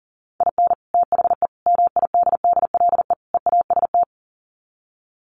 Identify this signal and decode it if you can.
Morse code, heard as a keyed tone, about 40 wpm, 700 Hz. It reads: IN THE MIDDLE EAST
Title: Sets of 4 Words 40wpm